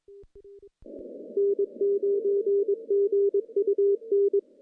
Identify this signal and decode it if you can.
Morse code, heard as a keyed tone, about 22 wpm, 400 Hz. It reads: N9GUN
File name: contests / CW-SS-k5tr- -1400